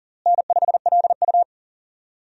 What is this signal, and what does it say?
Morse code, heard as a keyed tone, about 37 wpm, 700 Hz. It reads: N5LU
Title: USA Call Signs - 4 Characters - Repeated 3x - 40wpm